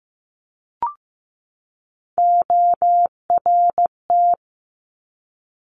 Morse code, heard as a keyed tone, about 15 wpm, 700 Hz. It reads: ORT
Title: Top Three Letter Combinations in Words 15wpm